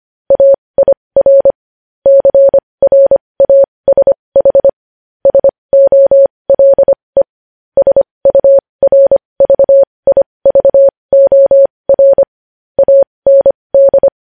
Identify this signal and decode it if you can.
Morse code, heard as a keyed tone, about 25 wpm, spaced as for 20 wpm, 550 Hz. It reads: AIR CRASH SOLE SURVIVOR AND
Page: News headlines in Morse code at 20 wpm